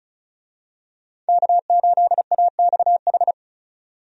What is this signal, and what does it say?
Morse code, heard as a keyed tone, about 34 wpm, 700 Hz. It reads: K8AXH
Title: USA Call Signs - 5 Characters - 35wpm